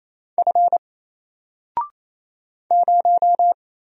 Morse code, heard as a keyed tone, about 28 wpm, 700 Hz. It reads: F 0
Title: Single Character (Letter:Number:Punctuation:Prosign) Alt 28wpm